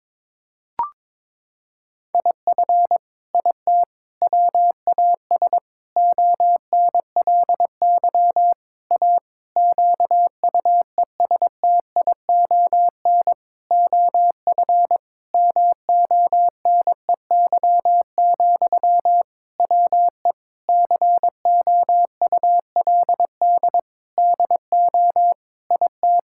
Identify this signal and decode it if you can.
Morse code, heard as a keyed tone, about 22 wpm, 700 Hz. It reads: IF IT WAS ONLY A QUESTION OF MONEY, WE COULD DO IT
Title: Sentences from Top 1600 Words 22wpm